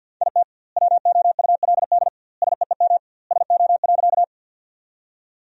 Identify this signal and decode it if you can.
Morse code, heard as a keyed tone, about 50 wpm, 700 Hz. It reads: IT WOULD SEEM SO.
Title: 2x Word-Spacing - Sentences from Top 300 Words - 50wpm